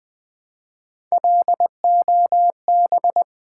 Morse code, heard as a keyed tone, about 20 wpm, 700 Hz. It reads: LOB
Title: Build A Word - 1 Letter At A Time - 3 to 5-Letter Words - 20wpm